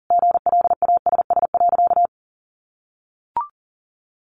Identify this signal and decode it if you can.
Morse code, heard as a keyed tone, about 40 wpm, 700 Hz. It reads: GLASS.
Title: Sentences from Top 1500 Words 40wpm